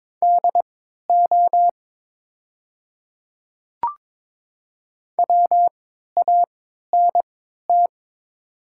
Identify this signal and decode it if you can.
Morse code, heard as a keyed tone, about 22 wpm, spaced as for 13 wpm, 700 Hz. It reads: DO WANT
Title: 3x Character Spacing - Top 200 Words - 22wpm